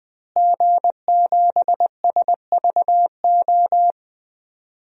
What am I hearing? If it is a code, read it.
Morse code, heard as a keyed tone, about 20 wpm, 700 Hz. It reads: G7SVO
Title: DXCC-100 Call Signs - 5 Characters - Repeated 3x - 20wpm